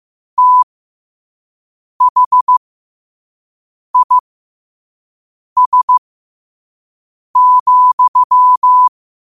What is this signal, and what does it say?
Morse code, heard as a keyed tone, about 15 wpm, spaced as for 5 wpm, 1000 Hz. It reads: THIS,
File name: qotdfiles / 5wpm